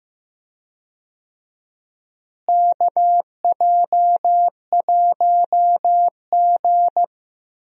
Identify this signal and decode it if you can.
Morse code, heard as a keyed tone, about 15 wpm, 700 Hz. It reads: KJ1G
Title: USA Call Signs - 4 Characters - Repeated 3x - 15wpm